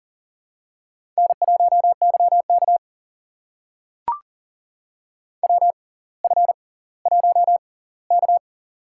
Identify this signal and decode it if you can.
Morse code, heard as a keyed tone, about 39 wpm, spaced as for 20 wpm, 690 Hz. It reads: N1YK WF1K
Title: USA Call Signs - 4 Characters - 6x Character Spacing - 40wpm